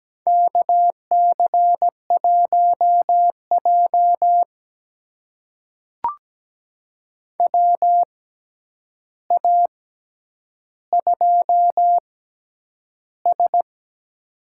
Morse code, heard as a keyed tone, about 17 wpm, spaced as for 9 wpm, 700 Hz. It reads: KC1J WA2S
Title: USA Call Signs - 4 Characters - 6x Character Spacing - 17wpm